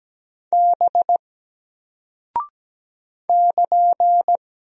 Morse code, heard as a keyed tone, about 17 wpm, 700 Hz. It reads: B <KN>
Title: Single Character (Letter:Number:Punctuation:Prosign) 17wpm